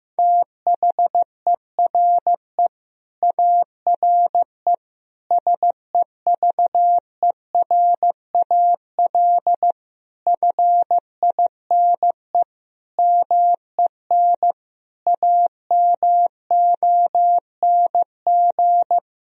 Morse code, heard as a keyed tone, about 15 wpm, 700 Hz. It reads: THERE ARE SEVERAL FINE MEN AMONG
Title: Sentences from Top 1000 Words - Review Part 2 - 15wpm